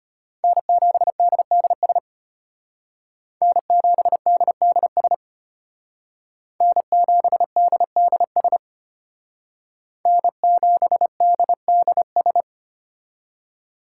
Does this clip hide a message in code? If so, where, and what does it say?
Yes: Morse code from 0.4 to 12.4 s, heard as a keyed tone, about 33 wpm, 700 Hz. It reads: N7DDS N7DDS N7DDS N7DDS